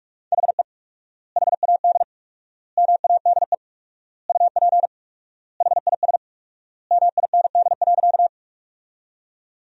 Morse code, heard as a keyed tone, about 47 wpm, 700 Hz. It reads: HE HAD MADE UP HIS MIND.
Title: 4x Word-Spacing - Sentences from Top 500 Words - 45wpm